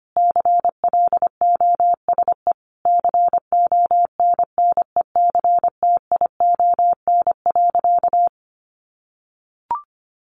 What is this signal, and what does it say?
Morse code, heard as a keyed tone, about 25 wpm, 700 Hz. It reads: CLOSE CONNECTION.
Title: Sentences from Top 1300 Words 25wpm